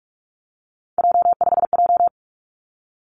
Morse code, heard as a keyed tone, about 46 wpm, 700 Hz. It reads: J5J
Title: Three Challenging Letter-Number Combinations 45wpm